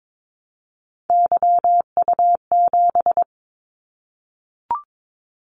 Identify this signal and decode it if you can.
Morse code, heard as a keyed tone, about 22 wpm, 700 Hz. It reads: YU7
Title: Callsign Prefixes for Top 25 DXCC - 22wpm